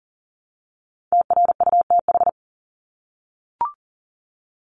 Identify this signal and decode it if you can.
Morse code, heard as a keyed tone, about 40 wpm, 700 Hz. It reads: TRUTH
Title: Top 751-1000 Words 40wpm